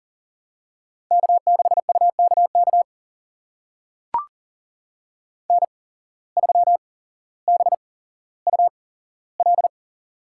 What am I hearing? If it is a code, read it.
Morse code, heard as a keyed tone, about 40 wpm, spaced as for 17 wpm, 700 Hz. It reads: K6UKK N3BUL
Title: USA Call Signs - 5 Characters - 8x Character Spacing - 40wpm